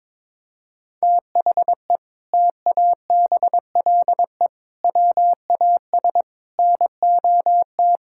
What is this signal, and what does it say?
Morse code, heard as a keyed tone, about 23 wpm, 700 Hz. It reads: THE TABLE WAS NOT
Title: Sentences from Top 1000 Words - Review Part 6 - 22wpm